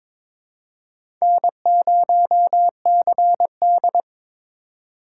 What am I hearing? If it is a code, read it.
Morse code, heard as a keyed tone, about 22 wpm, 700 Hz. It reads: N0CD
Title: USA Call Signs - 4 Characters - 2x Character Spacing - 22wpm